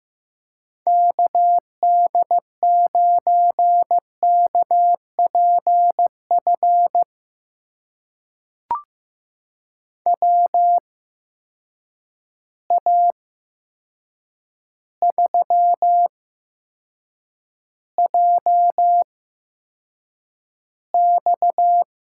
Morse code, heard as a keyed tone, about 15 wpm, spaced as for 7 wpm, 700 Hz. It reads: KD9KPF WA3JX
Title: USA Call Signs - 6 Characters - 8x Character Spacing - 15wpm